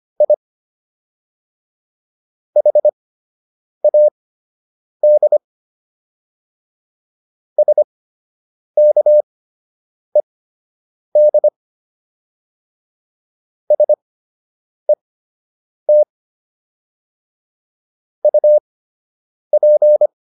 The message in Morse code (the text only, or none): I HAD SKED SET UP